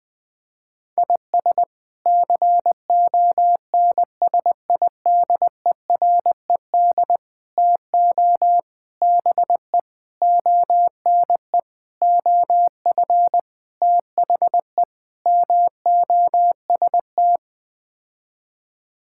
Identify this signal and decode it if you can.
Morse code, heard as a keyed tone, about 21 wpm, 700 Hz. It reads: IS CONSIDERED TO BE ONE OF THE MOST
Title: Sets of 8 Words - 20wpm